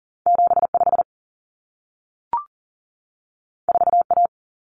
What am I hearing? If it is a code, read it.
Morse code, heard as a keyed tone, about 40 wpm, 700 Hz. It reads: 75 4A
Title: Common Cut Numbers - 2-Digit 40wpm